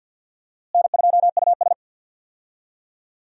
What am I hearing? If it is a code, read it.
Morse code, heard as a keyed tone, about 46 wpm, 700 Hz. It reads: N2US
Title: USA Call Signs - 4 Characters - Repeated 3x - 50wpm